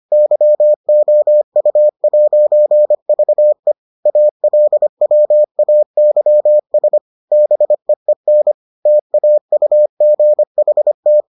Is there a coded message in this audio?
Yes: Morse code from 0.1 to 11.2 s, heard as a keyed tone, about 25 wpm, 600 Hz. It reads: YOU'VE ALWAYS BEEN TAUGHT